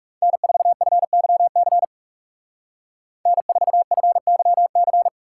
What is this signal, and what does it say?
Morse code, heard as a keyed tone, about 36 wpm, 700 Hz. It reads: N4FYC N4FYC
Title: Speed Racing - USA 1x3 Call Signs 30wpm